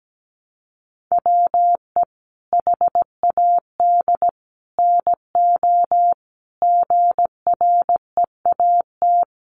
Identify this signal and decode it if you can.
Morse code, heard as a keyed tone, about 17 wpm, 700 Hz. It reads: WE HAD NO GREAT
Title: Sentences from Top 1200 Words 17wpm